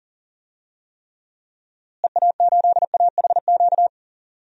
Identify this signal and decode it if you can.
Morse code, heard as a keyed tone, about 40 wpm, 700 Hz. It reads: EA8AHQ
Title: DXCC-100 Call Signs - 6 Characters - Repeated 3x - 40wpm